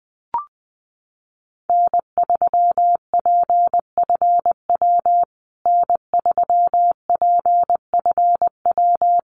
Morse code, heard as a keyed tone, about 20 wpm, 700 Hz. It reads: N3PFW N3PFW
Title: USA Call Signs Repeated 3x 20wpm